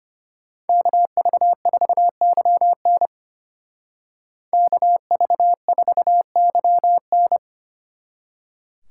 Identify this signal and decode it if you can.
Morse code, heard as a keyed tone, about 31 wpm, 700 Hz. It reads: KV4YN KV4YN
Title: Speed Racing - USA 2x2 Call Signs 25wpm